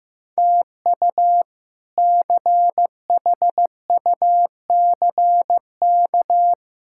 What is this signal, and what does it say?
Morse code, heard as a keyed tone, about 15 wpm, 700 Hz. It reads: TU CHUCK
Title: K1USN SST QSO Practice - 15wpm